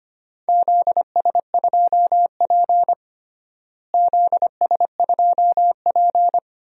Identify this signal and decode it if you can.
Morse code, heard as a keyed tone, about 25 wpm, 700 Hz. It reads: ZS2P ZS2P
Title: DXCC-100 Call Signs - 4 Characters - Repeated 2x - 25wpm